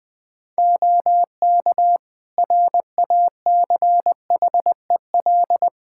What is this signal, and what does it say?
Morse code, heard as a keyed tone, about 20 wpm, 700 Hz. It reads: OK RACHEL
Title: Practice QSO - Exchange 2 - Info - 20wpm